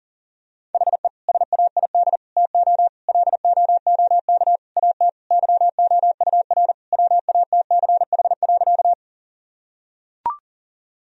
Morse code, heard as a keyed tone, about 40 wpm, 700 Hz. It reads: HE SAID TO LOOK AT YOUR WATCH.
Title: Sentences from Top 1000 Words - Review Part 1 - 40wpm